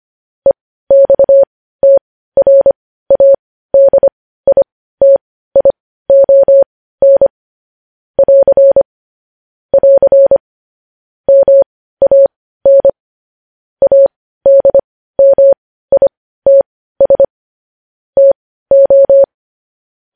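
Morse code, heard as a keyed tone, about 25 wpm, spaced as for 15 wpm, 550 Hz. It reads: EXTRADITION <AR> <AR> MAN ADMITS TO
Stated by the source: Morse code news headlines.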